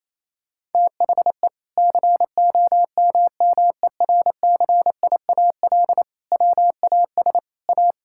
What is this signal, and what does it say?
Morse code, heard as a keyed tone, about 28 wpm, 700 Hz. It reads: THE COMMERCIAL WAS A
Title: Sentences from Top 1000 Words - Review Part 5 - 28wpm